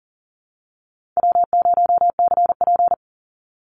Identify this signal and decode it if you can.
Morse code, heard as a keyed tone, about 40 wpm, 700 Hz. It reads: W0CP
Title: Easy Call Signs Repeated 3x - 40wpm